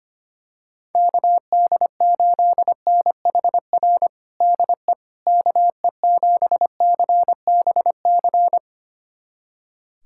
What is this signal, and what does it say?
Morse code, heard as a keyed tone, about 25 wpm, 700 Hz. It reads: KD8NHR DE KE7CBC
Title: USA Callsign DE Callsign 25wpm